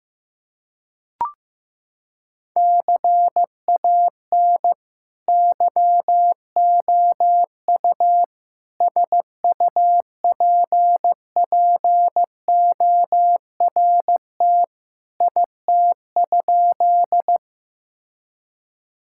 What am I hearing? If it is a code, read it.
Morse code, heard as a keyed tone, about 15 wpm, 700 Hz. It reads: CAN YOU SUPPORT IT?
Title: Sentences from Top 1000 Words - Review Part 5 - 15wpm